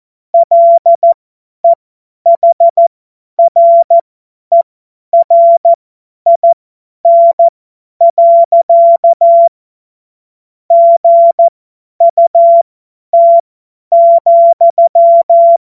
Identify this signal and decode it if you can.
Morse code, heard as a keyed tone, about 14 wpm, spaced as for 10 wpm, 680 Hz. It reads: LEHRERIN. GUT,
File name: Morsekurs / Witze / Witz 28. September 2021, 50 F70